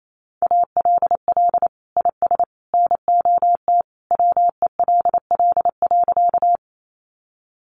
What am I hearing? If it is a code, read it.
Morse code, heard as a keyed tone, about 28 wpm, 700 Hz. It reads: ALL IS NOT WELL.
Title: Sentences from Top 600 Words 28wpm